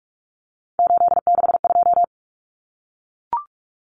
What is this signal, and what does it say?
Morse code, heard as a keyed tone, about 44 wpm, 700 Hz. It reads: 862
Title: Numbers - 3-Digit 45wpm